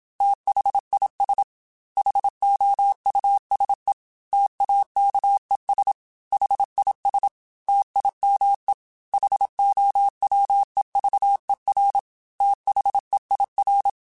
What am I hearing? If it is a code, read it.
Morse code, heard as a keyed tone, about 27 wpm, 800 Hz. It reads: THIS HOUSE TAKES HIS TIME HOWEVER THEIR
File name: T then H Runs / T then H Runs.mp3s / T then H Runs @25